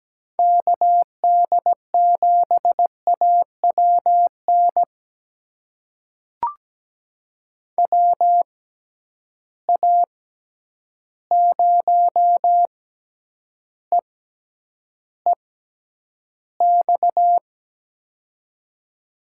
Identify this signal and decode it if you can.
Morse code, heard as a keyed tone, about 17 wpm, spaced as for 9 wpm, 700 Hz. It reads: KD7AWN WA0EEX
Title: USA Call Signs - 6 Characters - 6x Character Spacing - 17wpm